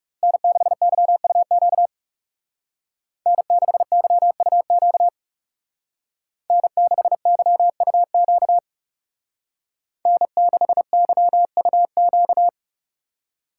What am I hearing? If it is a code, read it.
Morse code, heard as a keyed tone, about 39 wpm, 700 Hz. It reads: N6YUQ N6YUQ N6YUQ N6YUQ